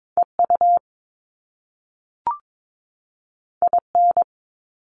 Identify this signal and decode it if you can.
Morse code, heard as a keyed tone, about 22 wpm, 700 Hz. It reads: EU IN
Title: Top Two Letter Combinations in Words 22wpm